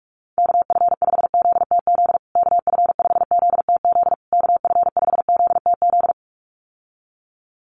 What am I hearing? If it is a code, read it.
Morse code, heard as a keyed tone, about 44 wpm, 700 Hz. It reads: KF5ZTZ KF5ZTZ KF5ZTZ
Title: USA Call Signs Repeated 3x 45wpm